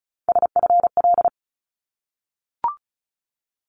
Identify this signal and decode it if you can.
Morse code, heard as a keyed tone, about 36 wpm, 700 Hz. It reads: SFL
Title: ARRL Sections 35wpm